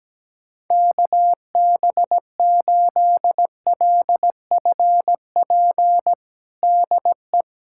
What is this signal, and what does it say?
Morse code, heard as a keyed tone, about 17 wpm, 700 Hz. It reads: KB8LFP DE
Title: USA Callsign DE Callsign - 6 Characters - 17wpm